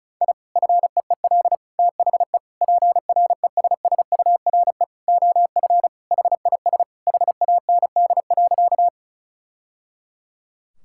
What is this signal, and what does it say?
Morse code, heard as a keyed tone, about 36 wpm, 700 Hz. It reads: I FEEL THE PRESSURE OF HIS HAND.
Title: Sentences from Top 1000 Words - Review Part 6 - 35wpm